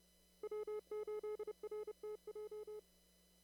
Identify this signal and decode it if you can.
Morse code, heard as a keyed tone, about 30 wpm, 420 Hz. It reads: W8RTJ
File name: CW / kt5j wpx cw-op-k5pi